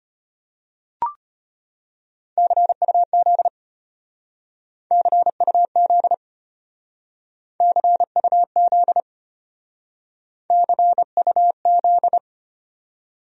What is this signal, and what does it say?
Morse code, heard as a keyed tone, about 33 wpm, 700 Hz. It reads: CUZ CUZ CUZ CUZ